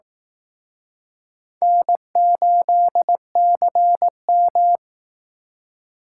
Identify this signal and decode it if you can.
Morse code, heard as a keyed tone, about 18 wpm, 700 Hz. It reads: N8CM
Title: Speed Racing - USA 1x2 Call Signs 15wpm